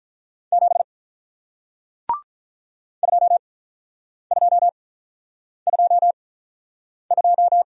Morse code, heard as a keyed tone, about 49 wpm, spaced as for 9 wpm, 700 Hz. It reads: 7 2222